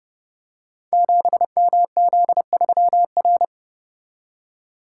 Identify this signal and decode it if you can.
Morse code, heard as a keyed tone, about 30 wpm, 700 Hz. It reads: 7MZ3R
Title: Code Groups - 5 Letter:Numbers - 30wpm